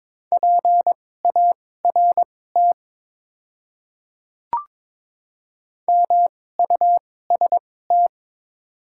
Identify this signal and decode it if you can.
Morse code, heard as a keyed tone, about 22 wpm, spaced as for 16 wpm, 700 Hz. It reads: PART MUST
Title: 2x Character Spacing - Top 300 Words - 22wpm